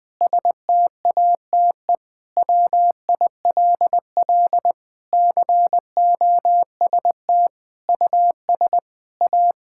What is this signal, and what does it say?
Morse code, heard as a keyed tone, about 20 wpm, 700 Hz. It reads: STATE WILL COST US A
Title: Sentences from Top 1000 Words - Review Part 5 - 20wpm